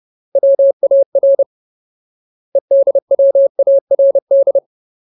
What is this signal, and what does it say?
Morse code, heard as a keyed tone, about 30 wpm, 550 Hz. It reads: WAR EDWARD
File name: Ultimate Word List / 30WPM Ultimate